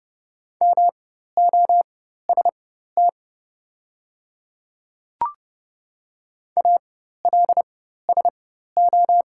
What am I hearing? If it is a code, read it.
Morse code, heard as a keyed tone, about 30 wpm, spaced as for 14 wpm, 700 Hz. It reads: MOST ALSO
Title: 4x Character Spacing - Top 200 Words - 30wpm